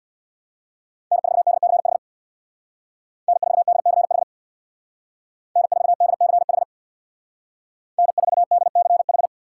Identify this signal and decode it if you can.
Morse code, heard as a keyed tone, about 61 wpm, 700 Hz. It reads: N4DCH N4DCH N4DCH N4DCH
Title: Speed Racing - USA 1x3 Call Signs 50wpm